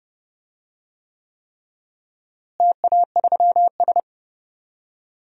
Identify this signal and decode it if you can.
Morse code, heard as a keyed tone, about 30 wpm, 700 Hz. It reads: TA3S
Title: DXCC-100 Call Signs - 4 Characters - Repeated 3x - 30wpm